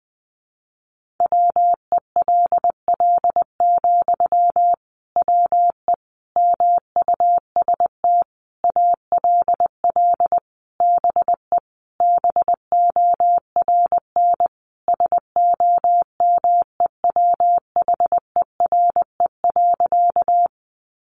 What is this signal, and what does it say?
Morse code, heard as a keyed tone, about 20 wpm, 700 Hz. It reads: WELL, WE MUST ALL BE BORN SOMEWHERE.
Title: Sentences from Top 1400 Words 20wpm